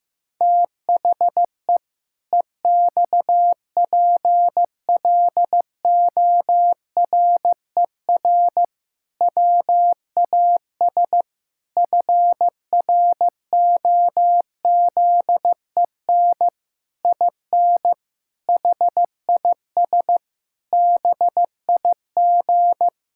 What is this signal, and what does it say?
Morse code, heard as a keyed tone, about 16 wpm, 700 Hz. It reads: THE EXPLORER WAS FROZEN IN HIS BIG
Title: Pangram Challenge v2 - 15wpm